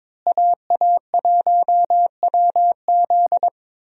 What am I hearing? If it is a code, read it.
Morse code, heard as a keyed tone, about 22 wpm, 700 Hz. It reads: AA1WZ
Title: USA Call Signs - 5 Characters - Repeated 2x - 22wpm